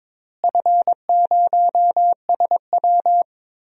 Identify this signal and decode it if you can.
Morse code, heard as a keyed tone, about 22 wpm, 700 Hz. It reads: F0SW
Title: DXCC-100 Call Signs - 4 Characters - Repeated 3x - 22wpm